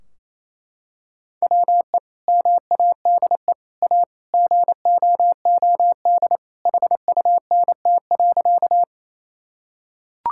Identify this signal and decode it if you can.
Morse code, heard as a keyed tone, about 27 wpm, 700 Hz. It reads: WE MADE A GOOD HUNT.
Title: Sentences from Top 1600 Words 28wpm